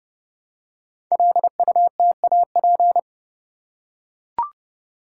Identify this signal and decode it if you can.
Morse code, heard as a keyed tone, about 30 wpm, 700 Hz. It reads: LUTAP
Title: Code Groups - 5 Letters - 30wpm